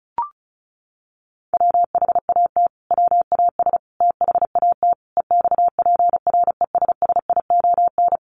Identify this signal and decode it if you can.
Morse code, heard as a keyed tone, about 34 wpm, 700 Hz. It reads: WHAT WAS THAT EXPRESSION
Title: Sentences from Top 1200 Words 35wpm